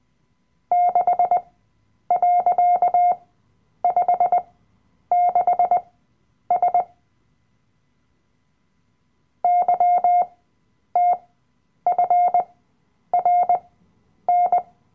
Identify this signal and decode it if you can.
Morse code, heard as a keyed tone, about 20 wpm, spaced as for 9 wpm, 710 Hz. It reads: 6.56S YTFRN
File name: Workshop Dateien / CW-Online-Kurs / Lektion 007